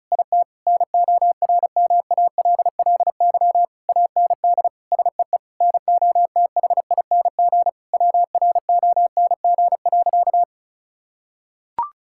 Morse code, heard as a keyed tone, about 36 wpm, 690 Hz. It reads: IT NORMALLY AND SEE NOTHING WRONG.
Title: Sentences from Top 1000 Words - Review Part 6 - 35wpm